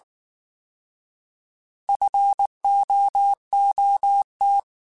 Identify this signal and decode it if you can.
Morse code, heard as a keyed tone, about 19 wpm, 800 Hz. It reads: FOOT